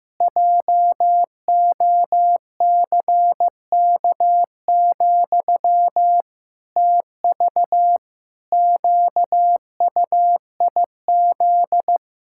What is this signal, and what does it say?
Morse code, heard as a keyed tone, about 15 wpm, 700 Hz. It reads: JOCK, TV QUIZ
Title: Pangram Challenge v2 - 15wpm